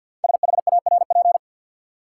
Morse code, heard as a keyed tone, about 53 wpm, 700 Hz. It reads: SHARP
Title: Sentences from Top 1600 Words - New Word Drill Alt - 50wpm